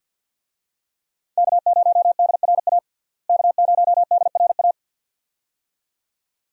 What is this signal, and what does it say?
Morse code, heard as a keyed tone, about 51 wpm, 700 Hz. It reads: K0DRA K0DRA
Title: USA Call Signs - 5 Characters - Repeated 2x - 50wpm